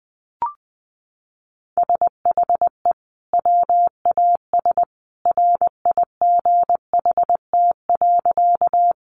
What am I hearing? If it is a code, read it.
Morse code, heard as a keyed tone, about 20 wpm, 700 Hz. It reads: SHE WAS RIGHT.
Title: Sentences from Top 200 Words 20wpm